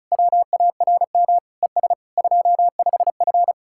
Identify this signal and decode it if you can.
Morse code, heard as a keyed tone, about 40 wpm, 700 Hz. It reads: WARM ES 25F